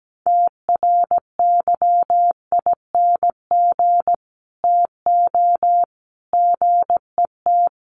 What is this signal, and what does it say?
Morse code, heard as a keyed tone, about 17 wpm, 700 Hz. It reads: TRYING TO GET
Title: Sets of 3 Words 17wpm